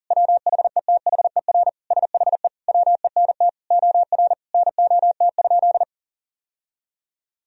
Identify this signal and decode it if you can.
Morse code, heard as a keyed tone, about 40 wpm, 700 Hz. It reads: WHETHER SHE WENT OR NOT?
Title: Sentences from Top 1000 Words - Review Part 3 - 40wpm